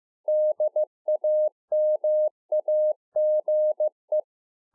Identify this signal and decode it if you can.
Morse code, heard as a keyed tone, about 15 wpm, 610 Hz. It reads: DAMAGE